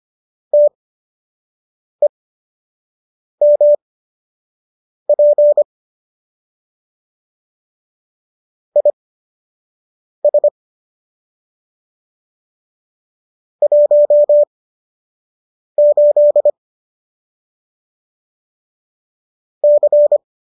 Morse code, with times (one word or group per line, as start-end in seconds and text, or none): 0.5-5.6 TEMP
8.8-10.5 IS
13.6-16.5 18
19.6-20.2 C